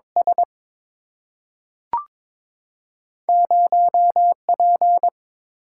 Morse code, heard as a keyed tone, about 22 wpm, 700 Hz. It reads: S 0P
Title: Single-Letter Single-Number 22wpm